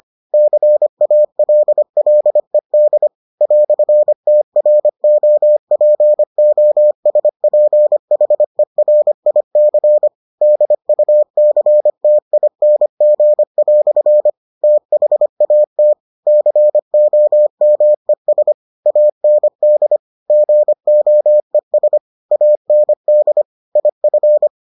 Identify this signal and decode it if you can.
Morse code, heard as a keyed tone, about 25 wpm, 600 Hz. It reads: CALLED "TROPOSPHERIC DUCTING" THAT COMES AND GOES AND IF